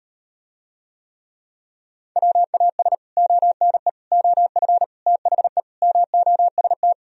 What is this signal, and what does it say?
Morse code, heard as a keyed tone, about 38 wpm, 700 Hz. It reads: WAS ONE OF THE MOST